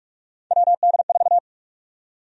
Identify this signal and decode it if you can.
Morse code, heard as a keyed tone, about 46 wpm, 700 Hz. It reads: WD4
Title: USA Call Sign Prefixes - 45wpm